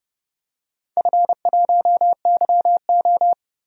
Morse code, heard as a keyed tone, about 29 wpm, 700 Hz. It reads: F1YO
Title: DXCC-100 Call Signs - 4 Characters - 30wpm